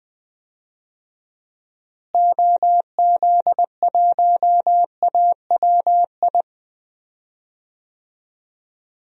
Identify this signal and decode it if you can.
Morse code, heard as a keyed tone, about 20 wpm, 700 Hz. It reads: OZ1AWI
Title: DXCC-100 Call Signs - 6 Characters - Repeated 3x - 20wpm